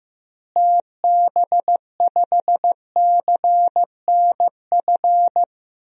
Morse code, heard as a keyed tone, about 15 wpm, 700 Hz. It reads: TB5CNF
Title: DXCC-100 Call Signs - 6 Characters - Repeated 2x - 15wpm